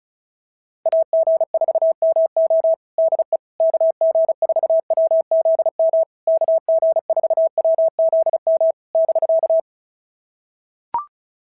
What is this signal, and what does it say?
Morse code, heard as a keyed tone, about 36 wpm, 650 Hz. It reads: AG4MO DE KG4WZM KG4WZM <BK>
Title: SOTA QSO Practice 35wpm